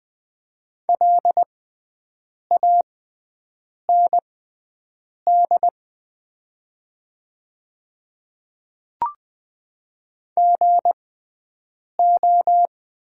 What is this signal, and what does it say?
Morse code, heard as a keyed tone, about 20 wpm, spaced as for 7 wpm, 700 Hz. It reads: LAND GO
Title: 6x Character Spacing - Top 300 Words - 20wpm